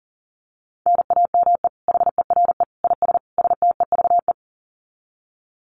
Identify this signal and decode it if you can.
Morse code, heard as a keyed tone, about 40 wpm, 700 Hz. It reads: NAME HERE IS STEVE
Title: Common QSO Phrases 40wpm